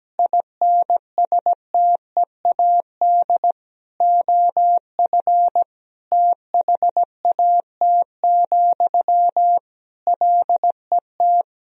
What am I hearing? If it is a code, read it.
Morse code, heard as a keyed tone, about 17 wpm, 700 Hz. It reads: INSTEAD OF THAT, LET
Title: Sentences from Top 1000 Words - Review Part 4 - 17wpm